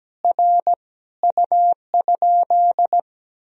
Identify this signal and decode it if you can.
Morse code, heard as a keyed tone, about 17 wpm, 700 Hz. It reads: R U?
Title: Sentences from Top 100 Words and Abbreviations 17wpm